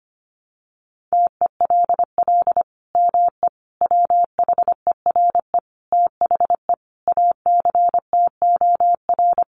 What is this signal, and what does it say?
Morse code, heard as a keyed tone, about 25 wpm, 700 Hz. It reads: TELL ME WHERE THE ACTOR
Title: Sentences from Top 1500 Words 25wpm